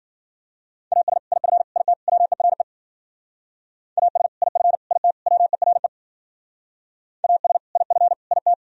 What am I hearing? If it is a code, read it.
Morse code, heard as a keyed tone, about 49 wpm, 700 Hz. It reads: AS IF IT WERE AS IF IT WERE AS IF IT
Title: Speed-Racing - Sets of 4 Words 40wpm